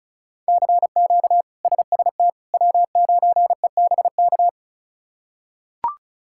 Morse code, heard as a keyed tone, about 34 wpm, 700 Hz. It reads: CQ SST W9EBK
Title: K1USN SST QSO Practice - 35wpm